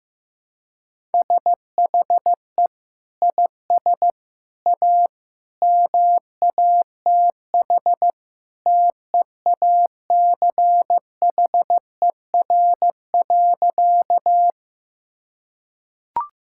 Morse code, heard as a keyed tone, about 16 wpm, 700 Hz. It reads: SHE IS A MATH TEACHER.
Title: Sentences from Top 1000 Words - Review Part 6 - 15wpm